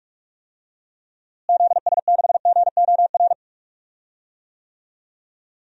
Morse code, heard as a keyed tone, about 45 wpm, 700 Hz. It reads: ZS6GOR
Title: DXCC-100 Call Signs - 6 Characters - Repeated 3x - 45wpm